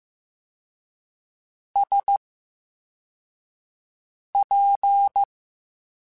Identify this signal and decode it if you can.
Morse code, heard as a keyed tone, about 15 wpm, spaced as for 4 wpm, 800 Hz. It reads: SP
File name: Special INGs / Special INGs-mp3s / Slow ing @13